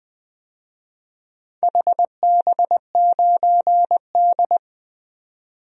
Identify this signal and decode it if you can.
Morse code, heard as a keyed tone, about 21 wpm, 700 Hz. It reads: HB9D
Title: DXCC-100 Call Signs - 4 Characters - 20wpm